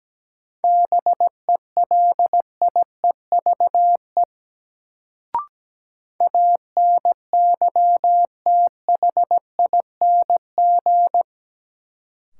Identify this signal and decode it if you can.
Morse code, heard as a keyed tone, about 17 wpm, 700 Hz. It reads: BELIEVE ANYTHING
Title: Sets of 4 Words - New Word Drill - 17wpm